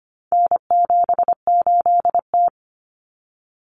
Morse code, heard as a keyed tone, about 25 wpm, 700 Hz. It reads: N78T
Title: Common Cut Numbers - 4-Digit 25wpm